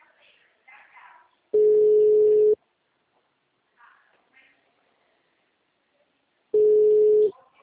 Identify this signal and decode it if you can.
Ringback tone (Europe, 425 Hz), with speech.